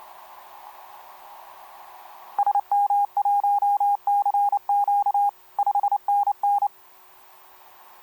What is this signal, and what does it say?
Morse code, heard as a keyed tone, about 28 wpm, 810 Hz. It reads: SM1CQ 5NN
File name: kp2-ks9w sm1cq etc 12m cw 13mar2011 1339